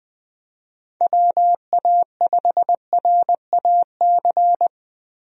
Morse code, heard as a keyed tone, about 21 wpm, 700 Hz. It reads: WA5RAC
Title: USA Call Signs - 6 Characters - 20wpm